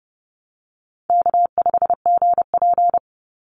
Morse code, heard as a keyed tone, about 30 wpm, 700 Hz. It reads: K5GP
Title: Call Signs Repeated 3x - 30wpm